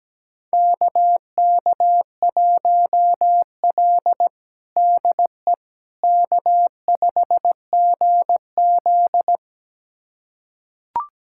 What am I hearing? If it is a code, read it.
Morse code, heard as a keyed tone, about 17 wpm, 700 Hz. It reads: KK1L DE K5GZ
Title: USA Callsign DE Callsign - 4 Characters - 17wpm